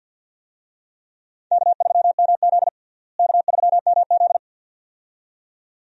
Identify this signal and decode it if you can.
Morse code, heard as a keyed tone, about 51 wpm, 700 Hz. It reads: K3MZ K3MZ
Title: USA Call Signs - 4 Characters - Repeated 2x - 50wpm